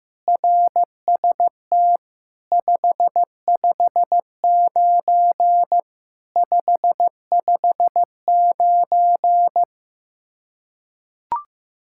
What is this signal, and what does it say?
Morse code, heard as a keyed tone, about 15 wpm, 700 Hz. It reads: RST 559 559
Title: Practice QSO - Exchange 1 - Beginning - 15wpm